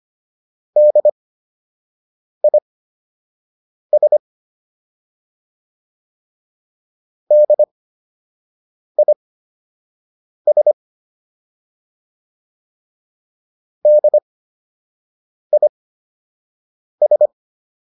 Morse code, heard as a keyed tone, about 25 wpm, spaced as for 6 wpm, 600 Hz. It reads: DIS DIS DIS
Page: fundamental-practice-files